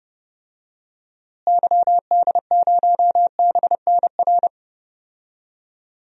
Morse code, heard as a keyed tone, about 30 wpm, 700 Hz. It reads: YD0BNR
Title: DXCC-100 Call Signs - 6 Characters - Repeated 3x - 30wpm